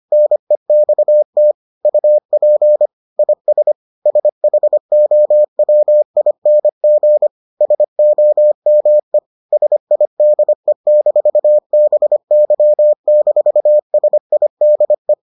Morse code, heard as a keyed tone, about 25 wpm, 600 Hz. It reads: NEXT UP IS SHOWING SOME SIDE-BY-SIDE